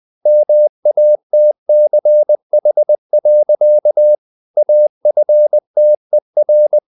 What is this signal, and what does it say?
Morse code, heard as a keyed tone, about 20 wpm, 600 Hz. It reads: MATCH. AFTER
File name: the old man and the sea part 2 20wpm